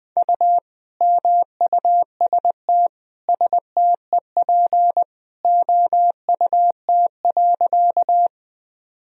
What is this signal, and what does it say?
Morse code, heard as a keyed tone, about 20 wpm, 700 Hz. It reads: U MUST STEP OUT.
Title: Sentences from Top 400 Words and Abbreviations 20wpm